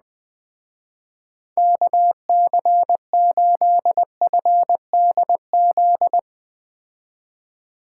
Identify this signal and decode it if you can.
Morse code, heard as a keyed tone, about 20 wpm, 700 Hz. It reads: KC8FDZ